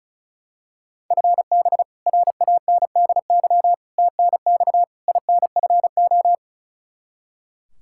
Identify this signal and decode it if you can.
Morse code, heard as a keyed tone, about 36 wpm, 700 Hz. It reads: FB RANDY TNX INFO
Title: Practice QSO - Exchange 3 - More Info - 35wpm